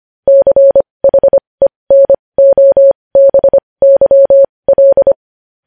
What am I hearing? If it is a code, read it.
Morse code, heard as a keyed tone, about 25 wpm, spaced as for 20 wpm, 550 Hz. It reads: CHENOBYL
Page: news headlines in morse code at 20 wpm